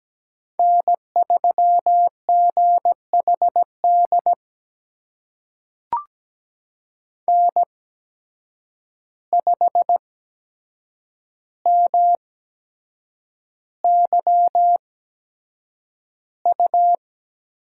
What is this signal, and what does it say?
Morse code, heard as a keyed tone, about 17 wpm, spaced as for 7 wpm, 700 Hz. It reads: N3GHD N5MYU
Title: USA Call Signs - 5 Characters - 8x Character Spacing - 17wpm